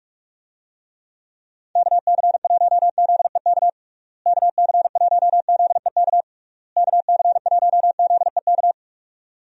Morse code, heard as a keyed tone, about 46 wpm, 700 Hz. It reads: KC1ZEK KC1ZEK KC1ZEK
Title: USA Call Signs - 6 Characters - Repeated 3x - 45wpm